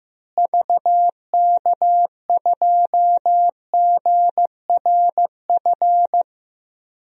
Morse code, heard as a keyed tone, about 15 wpm, 700 Hz. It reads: VK2GRF
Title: DXCC-100 Call Signs - 6 Characters - Repeated 3x - 15wpm